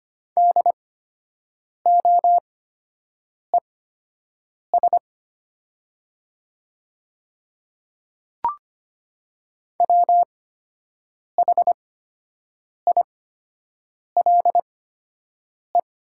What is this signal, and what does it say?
Morse code, heard as a keyed tone, about 25 wpm, spaced as for 7 wpm, 700 Hz. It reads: DOES WHILE
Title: 8x Character Spacing - Top 300 Words - 25wpm